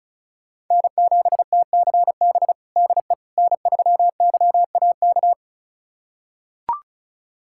Morse code, heard as a keyed tone, about 36 wpm, 700 Hz. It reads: N7TCB DE N3YAK
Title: USA Callsign DE Callsign - 5 Characters - 35wpm